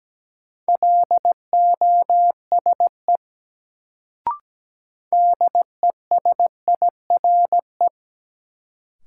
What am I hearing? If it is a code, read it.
Morse code, heard as a keyed tone, about 17 wpm, 700 Hz. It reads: LOSE DESIRE
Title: Sentences from Top 1100 Words - New Word Drill Alt - 17wpm